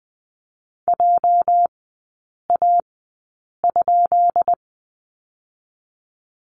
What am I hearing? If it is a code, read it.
Morse code, heard as a keyed tone, about 20 wpm, spaced as for 8 wpm, 690 Hz. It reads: JA?
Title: 205 Lesson 33e - Letter j - Three Characters 20wpm